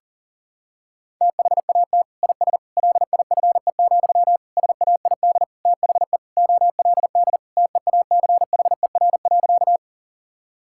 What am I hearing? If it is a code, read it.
Morse code, heard as a keyed tone, about 37 wpm, 700 Hz. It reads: THAT IS LIFE, SAID THE OLD TEACHER.
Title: Sentences from Top 1000 Words - Review Part 6 - 40wpm